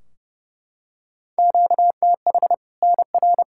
Morse code, heard as a keyed tone, about 30 wpm, 700 Hz. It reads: QTH NR
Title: Practice QSO - Exchange 1 - Beginning - 30wpm